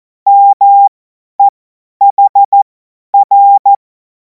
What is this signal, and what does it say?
Morse code, heard as a keyed tone, about 14 wpm, spaced as for 10 wpm, 800 Hz. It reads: MEHR
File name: Morsekurs / Witze / Witz 19. Mai 2026 50 F70